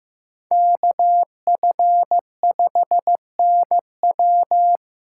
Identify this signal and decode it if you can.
Morse code, heard as a keyed tone, about 18 wpm, 700 Hz. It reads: KF5NW
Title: Speed Racing - USA 2x2 Call Signs 15wpm